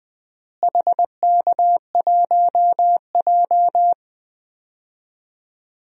Morse code, heard as a keyed tone, about 21 wpm, 700 Hz. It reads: HK1J
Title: DXCC-100 Call Signs - 4 Characters - Repeated 2x - 20wpm